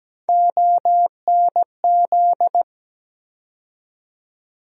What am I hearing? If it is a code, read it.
Morse code, heard as a keyed tone, about 17 wpm, 700 Hz. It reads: ONZ
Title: USA Call Sign Postfixes - 3 Letters 17wpm